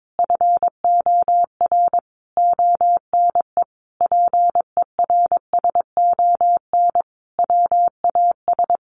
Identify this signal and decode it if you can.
Morse code, heard as a keyed tone, about 22 wpm, 700 Hz. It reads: FOR ONE PERSON WAS
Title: Sentences from Top 1400 Words 22wpm